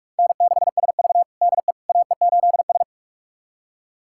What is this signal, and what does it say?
Morse code, heard as a keyed tone, about 46 wpm, 700 Hz. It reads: N6SV DE AE8S
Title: USA Callsign DE Callsign - 4 Characters - 45wpm